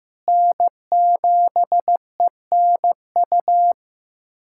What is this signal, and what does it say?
Morse code, heard as a keyed tone, about 15 wpm, 700 Hz. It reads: N7ENU
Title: USA Callsign DE Callsign - 5 Characters - 15wpm